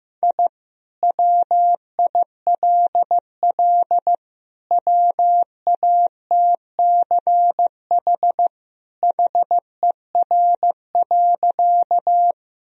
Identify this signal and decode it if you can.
Morse code, heard as a keyed tone, about 16 wpm, 700 Hz. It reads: I WILL WATCH HER.
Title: Sentences from Top 1000 Words - Review Part 1 - 15wpm